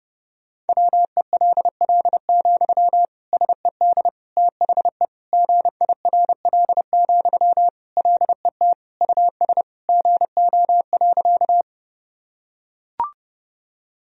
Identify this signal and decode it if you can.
Morse code, heard as a keyed tone, about 30 wpm, 700 Hz. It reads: WELL, SED THE GIRL, LET US GO.
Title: Sentences from Top 400 Words and Abbreviations 30wpm